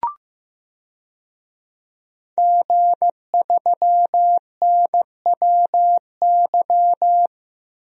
Morse code, heard as a keyed tone, about 15 wpm, 700 Hz. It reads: G3NWY
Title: DXCC-100 Call Signs - 5 Characters - 15wpm